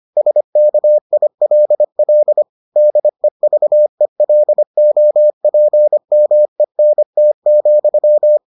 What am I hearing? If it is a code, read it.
Morse code, heard as a keyed tone, about 25 wpm, 600 Hz. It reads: SKILL DEVELOPMENT,